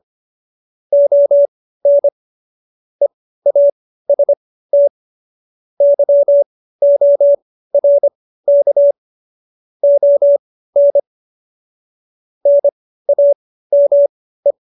Morse code, heard as a keyed tone, about 25 wpm, spaced as for 15 wpm, 570 Hz. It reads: ON EAST YORK ON NAME
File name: qso220b 15